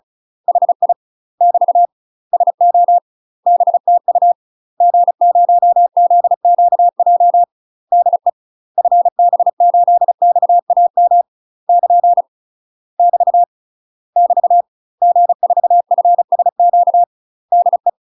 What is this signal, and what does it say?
Morse code, heard as a keyed tone, about 36 wpm, 700 Hz. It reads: HI <BT> SO BTU G0ZQJ DE FB8XAM <KN> <BT> <BT> G4FSQ DE
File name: UK MASTER QSOs TRAINING / 35WPM / UK MASTER QSOs 0016